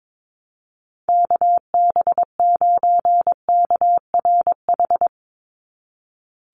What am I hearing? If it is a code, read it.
Morse code, heard as a keyed tone, about 22 wpm, 700 Hz. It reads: KB9KRH
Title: USA Call Signs 22wpm